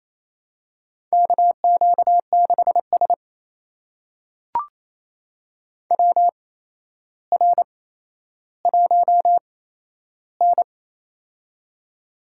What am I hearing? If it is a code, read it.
Morse code, heard as a keyed tone, about 28 wpm, spaced as for 12 wpm, 700 Hz. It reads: KQ6S WR1N
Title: USA Call Signs - 4 Characters - 8x Character Spacing - 28wpm